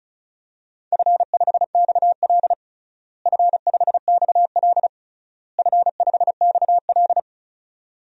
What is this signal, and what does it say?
Morse code, heard as a keyed tone, about 34 wpm, 700 Hz. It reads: F5XL F5XL F5XL
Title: DXCC-100 Call Signs - 4 Characters - Repeated 3x - 35wpm